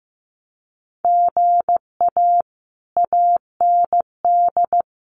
Morse code, heard as a keyed tone, about 15 wpm, 700 Hz. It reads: GA AND
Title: Common QSO Phrases Encore 15wpm